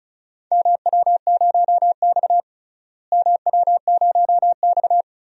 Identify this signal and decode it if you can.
Morse code, heard as a keyed tone, about 34 wpm, 700 Hz. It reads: MW0X MW0X
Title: DXCC-100 Call Signs - 4 Characters - Repeated 2x - 35wpm